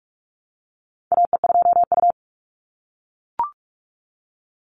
Morse code, heard as a keyed tone, about 44 wpm, 700 Hz. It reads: AE2U
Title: Call Signs - 45wpm